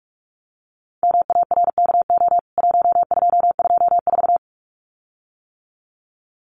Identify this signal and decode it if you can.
Morse code, heard as a keyed tone, about 44 wpm, 700 Hz. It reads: MARKO 1224
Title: CWT Exchange 45wpm